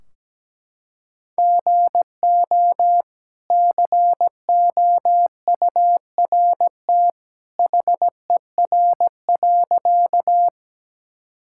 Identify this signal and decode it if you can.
Morse code, heard as a keyed tone, about 17 wpm, 700 Hz. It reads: GO COURT HER.
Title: Sentences from Top 1600 Words 17wpm